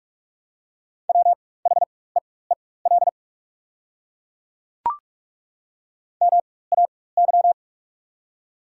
Morse code, heard as a keyed tone, about 48 wpm, spaced as for 21 wpm, 700 Hz. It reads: WHEEL MAY
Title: 4x Character Spacing - Top 500 Words - 45wpm